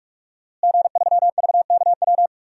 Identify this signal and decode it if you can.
Morse code, heard as a keyed tone, about 46 wpm, 700 Hz. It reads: G3VKW
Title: DXCC-100 Call Signs - 5 Characters - Repeated 2x - 45wpm